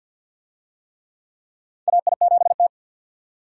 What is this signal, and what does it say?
Morse code, heard as a keyed tone, about 50 wpm, 700 Hz. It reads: AI7T
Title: USA Call Signs - 4 Characters - 50wpm